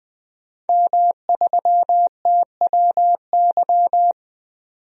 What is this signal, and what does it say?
Morse code, heard as a keyed tone, about 21 wpm, 700 Hz. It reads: M3TWY
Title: DXCC-100 Call Signs - 5 Characters - 20wpm